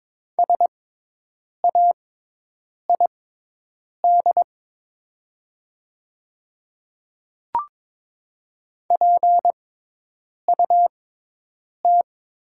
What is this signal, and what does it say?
Morse code, heard as a keyed tone, about 22 wpm, spaced as for 8 wpm, 700 Hz. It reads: SAID PUT
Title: 6x Character Spacing - Top 200 Words - 22wpm